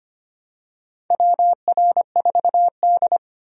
Morse code, heard as a keyed tone, about 25 wpm, 700 Hz. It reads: WR4D
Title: USA Call Signs - 4 Characters - 25wpm